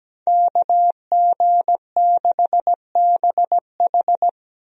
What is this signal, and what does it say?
Morse code, heard as a keyed tone, about 17 wpm, 700 Hz. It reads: KG6BH
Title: USA Call Signs - 5 Characters - Repeated 2x - 17wpm